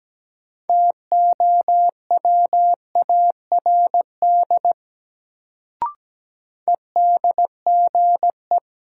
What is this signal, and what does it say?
Morse code, heard as a keyed tone, about 17 wpm, 700 Hz. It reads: TOWARD EDGE
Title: Sentences from Top 1300 Words - New Word Drill Alt - 17wpm